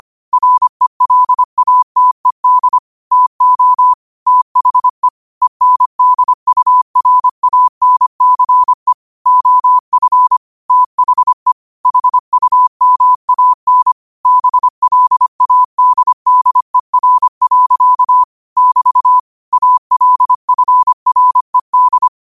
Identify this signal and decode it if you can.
Morse code, heard as a keyed tone, about 25 wpm, 1000 Hz. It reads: RELATED TO THE ENDURANCE OF THE HUMAN BLADDER. <BT> ALFRED